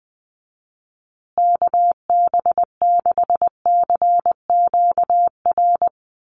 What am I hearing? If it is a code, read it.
Morse code, heard as a keyed tone, about 20 wpm, 700 Hz. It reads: KB6CQR
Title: USA Call Signs Repeated 3x 20wpm